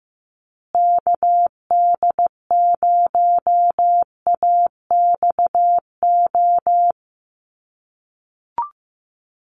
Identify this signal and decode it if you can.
Morse code, heard as a keyed tone, about 15 wpm, 700 Hz. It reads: KD0AXO
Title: USA Call Signs Repeated 2x 15wpm